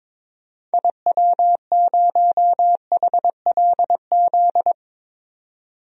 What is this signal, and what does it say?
Morse code, heard as a keyed tone, about 22 wpm, 700 Hz. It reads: IW0HLZ
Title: DXCC-100 Call Signs - 6 Characters - 22wpm